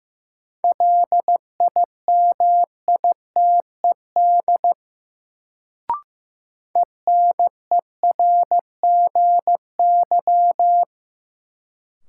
Morse code, heard as a keyed tone, about 15 wpm, 700 Hz. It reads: LIMITED ENERGY
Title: Sentences from Top 900 Words - New Word Drill - 15wpm